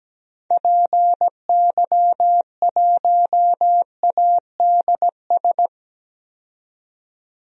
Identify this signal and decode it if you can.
Morse code, heard as a keyed tone, about 17 wpm, 700 Hz. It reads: PY1ADS
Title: DXCC-100 Call Signs - 6 Characters - 17wpm